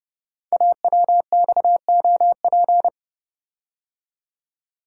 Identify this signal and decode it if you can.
Morse code, heard as a keyed tone, about 30 wpm, 700 Hz. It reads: AWXOP
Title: Code Groups - 5 Letters - 30wpm